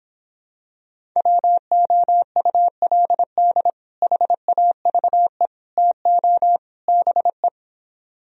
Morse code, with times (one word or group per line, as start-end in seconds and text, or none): 1.2-3.7 WOULD
4.0-5.5 HAVE
5.8-6.6 TO
6.9-7.5 BE